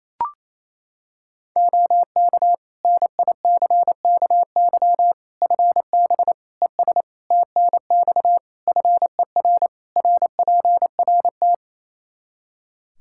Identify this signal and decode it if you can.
Morse code, heard as a keyed tone, about 28 wpm, 700 Hz. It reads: OK NICKY FB ES TNX FER RPRT
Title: Practice QSO - Exchange 2 - Info - 28wpm